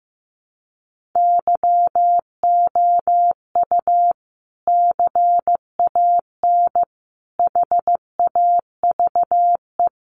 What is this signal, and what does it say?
Morse code, heard as a keyed tone, about 15 wpm, 700 Hz. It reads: YOU CAN HAVE
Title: Sentences from Top 100 Words 15wpm